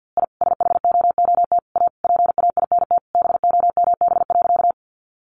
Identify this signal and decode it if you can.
Morse code, heard as a keyed tone, about 50 wpm, 700 Hz. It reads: I SHOOT A PAINT BOMB.
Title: A Sound Of Thunder 50wpm